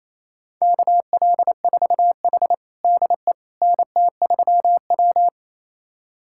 Morse code, heard as a keyed tone, about 28 wpm, 700 Hz. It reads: KL4H DE NT3W
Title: USA Callsign DE Callsign - 4 Characters - 28wpm